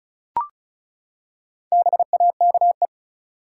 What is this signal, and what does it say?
Morse code, heard as a keyed tone, about 34 wpm, 700 Hz. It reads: BAKE
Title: Four Letter Words - 35wpm